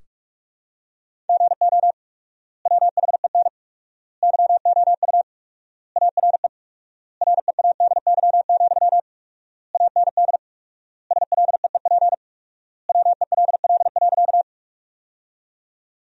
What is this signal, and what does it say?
Morse code, heard as a keyed tone, about 41 wpm, 690 Hz. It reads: GO WHEN YOU ARE READY, AND SLEEP WELL.
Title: 4x Word-Spacing - Sentences from Top 500 Words - 45wpm